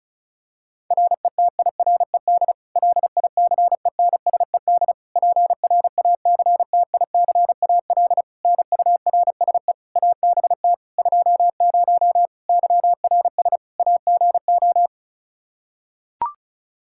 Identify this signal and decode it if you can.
Morse code, heard as a keyed tone, about 34 wpm, 700 Hz. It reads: RETIRED LICENSED PRACTICAL NURSE ABT 20 YRS AGO
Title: Practice QSO - Exchange 3 - More Info - 35wpm